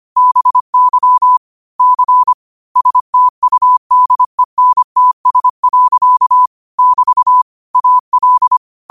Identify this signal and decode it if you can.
Morse code, heard as a keyed tone, about 25 wpm, 1000 Hz. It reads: BY C STUDENTS. <BT> AL
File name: qotdfiles / 25wpm